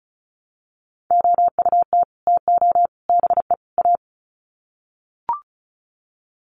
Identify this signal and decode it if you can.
Morse code, heard as a keyed tone, about 34 wpm, 700 Hz. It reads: OUT TO BE A
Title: Sets of 4 Words 35wpm